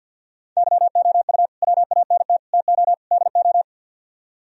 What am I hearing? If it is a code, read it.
Morse code, heard as a keyed tone, about 60 wpm, 700 Hz. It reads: YOU WANT TO DO
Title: Speed-Racing - Sets of 4 Words 50wpm